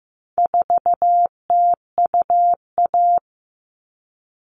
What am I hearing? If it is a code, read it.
Morse code, heard as a keyed tone, about 15 wpm, 700 Hz. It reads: 4TUA
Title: All Cut Numbers - 4-Digit 15wpm